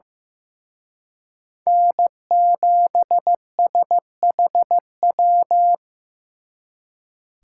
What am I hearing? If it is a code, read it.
Morse code, heard as a keyed tone, about 18 wpm, 700 Hz. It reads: N7SHW
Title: Speed Racing - USA 1x3 Call Signs 15wpm